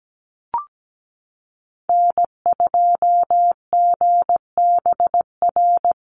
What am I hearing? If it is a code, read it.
Morse code, heard as a keyed tone, about 17 wpm, 700 Hz. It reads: N2GBR
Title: NA SOTA Activator Callsigns 17wpm